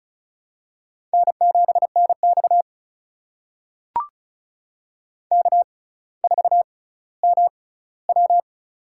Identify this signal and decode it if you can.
Morse code, heard as a keyed tone, about 36 wpm, spaced as for 18 wpm, 700 Hz. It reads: N7NX K4MW
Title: USA Call Signs - 4 Characters - 6x Character Spacing - 35wpm